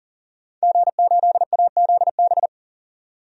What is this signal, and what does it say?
Morse code, heard as a keyed tone, about 40 wpm, 700 Hz. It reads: G8AZB
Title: DXCC-100 Call Signs - 5 Characters - 40wpm